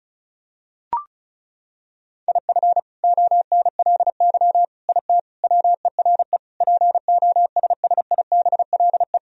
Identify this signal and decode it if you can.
Morse code, heard as a keyed tone, about 36 wpm, 700 Hz. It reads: IF ONLY IT WERE POSSIBLE
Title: Sentences from Top 500 Words and Abbreviations 35wpm